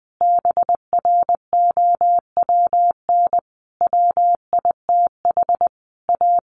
Morse code, heard as a keyed tone, about 20 wpm, 700 Hz. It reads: BROWN WITH A
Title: Sentences from Top 1400 Words 20wpm